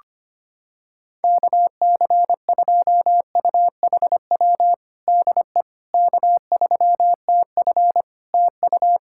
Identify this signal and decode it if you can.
Morse code, heard as a keyed tone, about 25 wpm, 700 Hz. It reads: KC2UHW DE K3TF TU
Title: Practice QSO - Exchange 4 - Ending - 25wpm